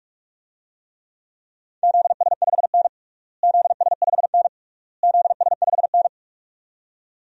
Morse code, heard as a keyed tone, about 41 wpm, 700 Hz. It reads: ZS5N ZS5N ZS5N
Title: DXCC-100 Call Signs - 4 Characters - Repeated 3x - 45wpm